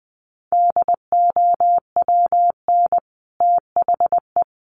Morse code, heard as a keyed tone, about 20 wpm, 700 Hz. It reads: DOWN THE
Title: Sets of 2 Words 20wpm